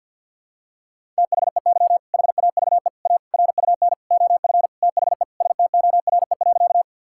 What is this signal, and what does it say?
Morse code, heard as a keyed tone, about 48 wpm, 690 Hz. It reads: THEY HAVE A RUN OF THE STORE.
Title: Sentences from Top 1000 Words - Review Part 5 - 50wpm